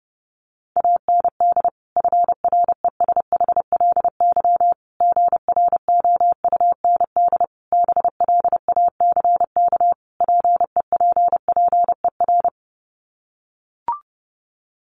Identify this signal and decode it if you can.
Morse code, heard as a keyed tone, about 30 wpm, 700 Hz. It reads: AND FRESHLY GROUND BLACK PEPPER
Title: Sets of 5 Words Encore 30wpm